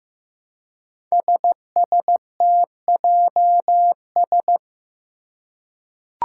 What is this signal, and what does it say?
Morse code, heard as a keyed tone, about 15 wpm, 700 Hz. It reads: SSTJS
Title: Five Challenging Letter-Number Combinations 15wpm